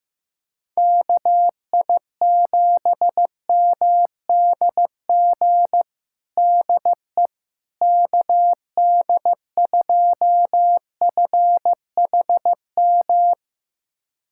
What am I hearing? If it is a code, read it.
Morse code, heard as a keyed tone, about 15 wpm, 700 Hz. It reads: KI7MDG DE KD2FHM
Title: USA Callsign DE Callsign 15wpm